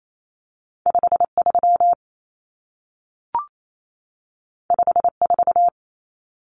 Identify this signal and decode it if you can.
Morse code, heard as a keyed tone, about 28 wpm, 700 Hz. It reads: 53 54
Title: Numbers - 2-Digit 28wpm